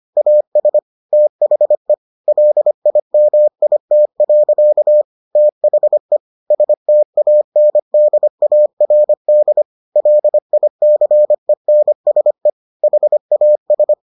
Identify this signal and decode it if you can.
Morse code, heard as a keyed tone, about 25 wpm, 600 Hz. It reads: AS THE LIMIT. THE STANDARD LICENSE HAS